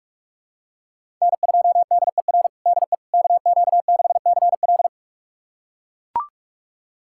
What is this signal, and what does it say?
Morse code, heard as a keyed tone, about 42 wpm, 700 Hz. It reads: N2DER DE KQ6CL
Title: USA Callsign DE Callsign - 5 Characters - 45wpm